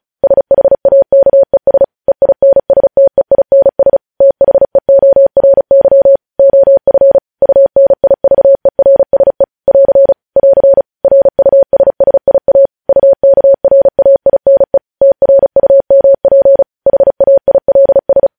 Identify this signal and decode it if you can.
Morse code, heard as a keyed tone, about 36 wpm, 550 Hz. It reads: SHAKES EINSTEINS THEORY OF UNIVERSE <AR> <AR> RUSSIA UKRAINE TRUMP HAILS
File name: podcast / News headlines-1x-35x35